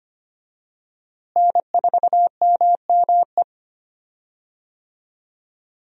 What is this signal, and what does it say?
Morse code, heard as a keyed tone, about 25 wpm, 700 Hz. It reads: N4MME
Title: USA Call Signs - 5 Characters - Repeated 3x - 25wpm